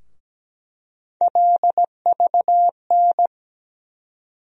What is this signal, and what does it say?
Morse code, heard as a keyed tone, about 17 wpm, 700 Hz. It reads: LVN
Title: USA Call Sign Postfixes - 3 Letters 17wpm